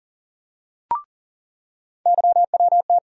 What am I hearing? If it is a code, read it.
Morse code, heard as a keyed tone, about 40 wpm, 700 Hz. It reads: YWT
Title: USA Call Sign Postfixes - 3 Letters 40wpm